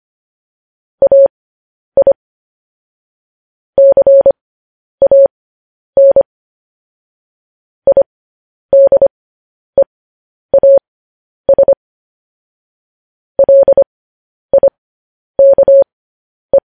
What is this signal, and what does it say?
Morse code, heard as a keyed tone, about 25 wpm, spaced as for 10 wpm, 550 Hz. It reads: AI CAN IDEAS LIKE
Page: news-headlines-in-morse-code-at-10-wpm